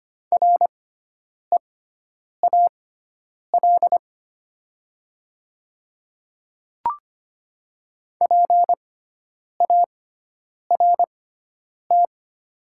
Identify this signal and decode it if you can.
Morse code, heard as a keyed tone, about 25 wpm, spaced as for 9 wpm, 700 Hz. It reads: REAL PART
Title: 6x Character Spacing - Top 300 Words - 25wpm